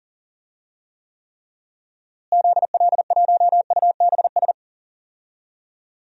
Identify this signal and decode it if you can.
Morse code, heard as a keyed tone, about 40 wpm, 700 Hz. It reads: ZL1UBS
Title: DXCC-100 Call Signs - 6 Characters - Repeated 3x - 40wpm